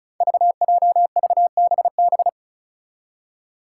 Morse code, heard as a keyed tone, about 34 wpm, 700 Hz. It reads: VJVBB
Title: Five Challenging Letter-Number Combinations 35wpm